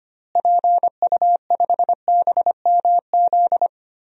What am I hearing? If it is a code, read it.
Morse code, heard as a keyed tone, about 25 wpm, 700 Hz. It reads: PU5BMZ
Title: DXCC-100 Call Signs - 6 Characters - Repeated 3x - 25wpm